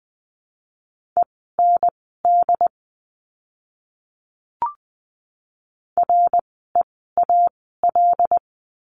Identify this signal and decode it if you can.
Morse code, heard as a keyed tone, about 20 wpm, spaced as for 14 wpm, 700 Hz. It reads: END REAL
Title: 075 Lesson 13f - Letter L - Words 20wpm